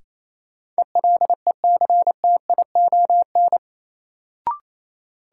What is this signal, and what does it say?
Morse code, heard as a keyed tone, about 28 wpm, 700 Hz. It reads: ELECTION
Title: Sentences from Top 1500 Words - New Word Drill - 28wpm